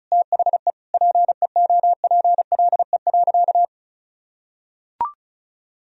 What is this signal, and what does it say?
Morse code, heard as a keyed tone, about 36 wpm, 700 Hz. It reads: THE PEOPLE.
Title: Sentences from Top 400 Words and Abbreviations 35wpm